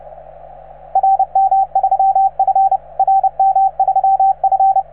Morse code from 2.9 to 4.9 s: RM3F